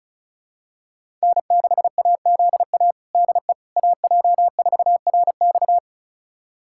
Morse code, heard as a keyed tone, about 36 wpm, 700 Hz. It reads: N6AZA DE AJ4RX
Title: USA Callsign DE Callsign - 5 Characters - 35wpm